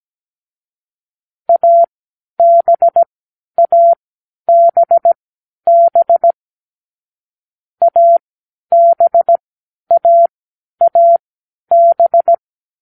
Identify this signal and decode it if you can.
Morse code, heard as a keyed tone, about 17 wpm, spaced as for 10 wpm, 680 Hz. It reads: ABABB ABAAB